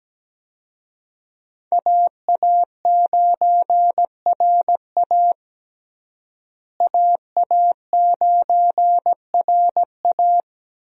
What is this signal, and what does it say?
Morse code, heard as a keyed tone, about 17 wpm, 700 Hz. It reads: AA9RA AA9RA
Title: USA Call Signs - 5 Characters - Repeated 2x - 17wpm